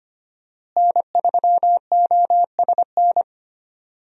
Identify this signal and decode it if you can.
Morse code, heard as a keyed tone, about 25 wpm, 700 Hz. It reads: N3OSN
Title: USA Call Signs - 5 Characters - Repeated 2x - 25wpm